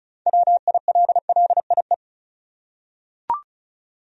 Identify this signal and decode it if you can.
Morse code, heard as a keyed tone, about 36 wpm, 700 Hz. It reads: WILLIE
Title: USA Common Names - 35wpm